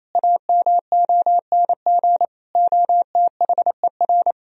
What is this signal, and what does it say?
Morse code, heard as a keyed tone, about 28 wpm, 700 Hz. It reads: AMONG OTHER
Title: Sentences from Top 1000 Words - Review Part 3 - 28wpm